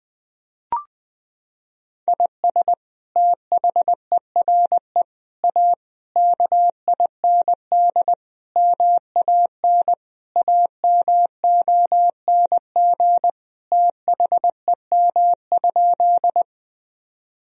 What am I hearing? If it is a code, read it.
Morse code, heard as a keyed tone, about 20 wpm, 700 Hz. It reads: IS THERE A KIND MAN AMONG THEM?
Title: Sentences from Top 1000 Words - Review Part 3 - 20wpm